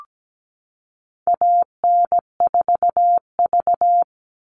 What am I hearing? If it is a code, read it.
Morse code, heard as a keyed tone, about 17 wpm, 700 Hz. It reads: AN4V
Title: All Cut Numbers - 4-Digit 17wpm